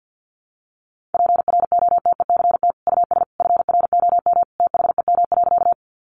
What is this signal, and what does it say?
Morse code, heard as a keyed tone, about 49 wpm, 700 Hz. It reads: PROTECT US FROM THEM.
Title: Sentences from Top 900 Words 50wpm